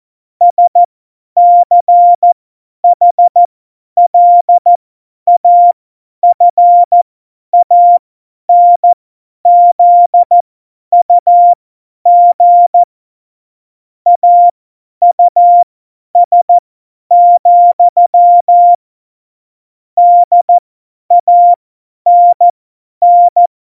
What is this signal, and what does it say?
Morse code, heard as a keyed tone, about 14 wpm, spaced as for 10 wpm, 700 Hz. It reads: SCHLAFANZUG AUS, DANN
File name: Morsekurs / Witze / Witz 15. Dezember 2020, 50 F70